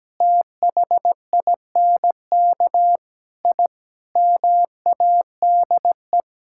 Morse code, heard as a keyed tone, about 17 wpm, 700 Hz. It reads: THINK I MADE
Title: Sentences from Top 1000 Words - Review Part 6 - 17wpm